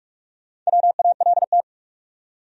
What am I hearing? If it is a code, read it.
Morse code, heard as a keyed tone, about 47 wpm, 700 Hz. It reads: WALT
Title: USA Common Names - Max 5 Letters - 45wpm